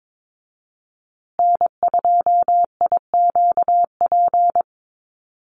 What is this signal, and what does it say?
Morse code, heard as a keyed tone, about 22 wpm, 700 Hz. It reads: N2IQP
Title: USA Call Signs Repeated 3x 22wpm